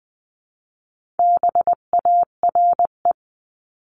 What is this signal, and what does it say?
Morse code, heard as a keyed tone, about 20 wpm, 700 Hz. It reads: BARE
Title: 259 Lesson 42d - Words 18-20wpm